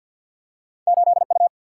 Morse code, heard as a keyed tone, about 53 wpm, 700 Hz. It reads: 8 U
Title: Single-Letter Single-Number 50wpm